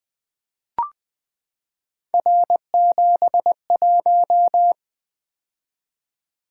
Morse code, heard as a keyed tone, about 20 wpm, 700 Hz. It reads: R71
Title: Three Letter-Number Combinations - 20wpm